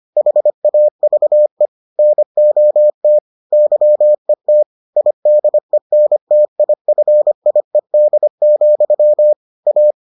Morse code, heard as a keyed tone, about 25 wpm, 600 Hz. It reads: HAVE NOT YET IDENTIFIED, A